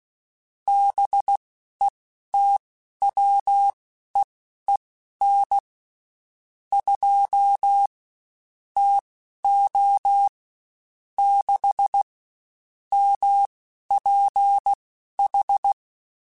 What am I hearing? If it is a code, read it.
Morse code, heard as a keyed tone, about 16 wpm, spaced as for 11 wpm, 800 Hz. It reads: BETWEEN 2 TO 6 MPH
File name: Spaced Characters / Spaced Characters-mp3s / Weather4 10 sp @15